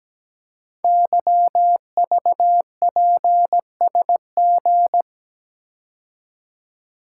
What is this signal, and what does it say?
Morse code, heard as a keyed tone, about 17 wpm, 700 Hz. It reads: YVPSG
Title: Code Groups - 5 Letters - 17wpm